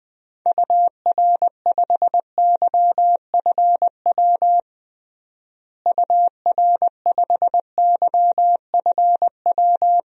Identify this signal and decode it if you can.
Morse code, heard as a keyed tone, about 20 wpm, 700 Hz. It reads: UR5YFW UR5YFW
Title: DXCC-100 Call Signs - 6 Characters - Repeated 2x - 20wpm